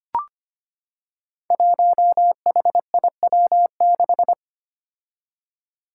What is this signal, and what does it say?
Morse code, heard as a keyed tone, about 25 wpm, 700 Hz. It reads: 1HIW6
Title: Five Challenging Letter-Number Combinations 25wpm